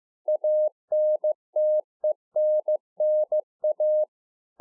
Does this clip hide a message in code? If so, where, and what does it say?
Yes: Morse code from 0.3 to 4.0 s, heard as a keyed tone, about 15 wpm, 610 Hz. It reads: ANTENNA